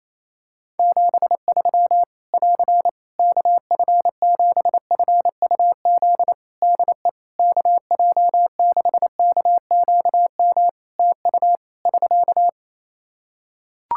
Morse code, heard as a keyed tone, about 28 wpm, 700 Hz. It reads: 73 <AR> KF7FUZ DE KJ6KQM TU <SK>
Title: Practice QSO - Exchange 4 - Ending - 28wpm